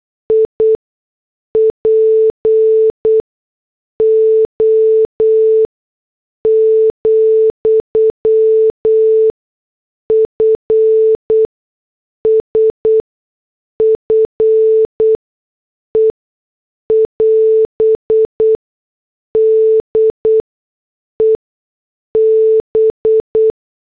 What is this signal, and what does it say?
Morse code, heard as a keyed tone, about 8 wpm, spaced as for 6 wpm, 440 Hz. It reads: IPO, FSFE <AS> DEB
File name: morse / headlines